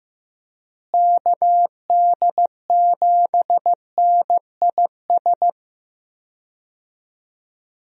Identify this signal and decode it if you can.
Morse code, heard as a keyed tone, about 15 wpm, 700 Hz. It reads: KD7NIS
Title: USA Call Signs - 6 Characters - Repeated 2x - 15wpm